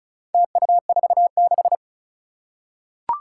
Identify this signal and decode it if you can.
Morse code, heard as a keyed tone, about 36 wpm, 700 Hz. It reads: TU46
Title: Four Challenging Letter-Number Combinations 35wpm